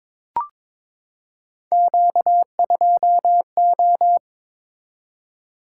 Morse code, heard as a keyed tone, about 22 wpm, 700 Hz. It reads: Q2O
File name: practice / 22wpm / Three Letter-Number Combinations - 22wpm